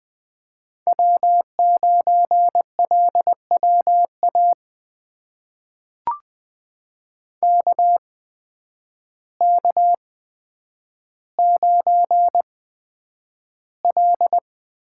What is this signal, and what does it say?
Morse code, heard as a keyed tone, about 20 wpm, spaced as for 9 wpm, 700 Hz. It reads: W9LWA KK9L
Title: USA Call Signs - 8x Character Spacing - 20wpm